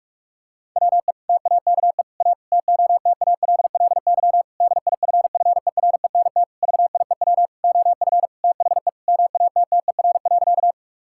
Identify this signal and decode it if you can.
Morse code, heard as a keyed tone, about 47 wpm, 700 Hz. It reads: WE TAKE A TOTALLY DIFFERENT VIEW OF THE MATTER.
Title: Sentences from Top 1600 Words 45wpm